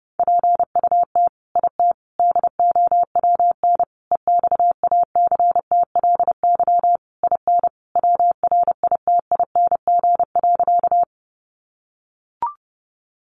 Morse code, heard as a keyed tone, about 30 wpm, 700 Hz. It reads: PUT IT DOWN EXACTLY IN WRITING.
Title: Sentences from Top 900 Words 30wpm